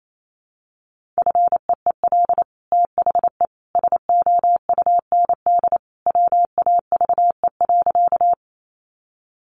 Morse code, heard as a keyed tone, about 28 wpm, 700 Hz. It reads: FEEL THE SOUND WAVE.
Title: Sentences from Top 1400 Words 28wpm